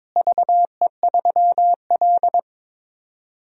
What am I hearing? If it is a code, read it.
Morse code, heard as a keyed tone, about 22 wpm, 700 Hz. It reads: VE3L
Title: DXCC-100 Call Signs - 4 Characters - Repeated 2x - 22wpm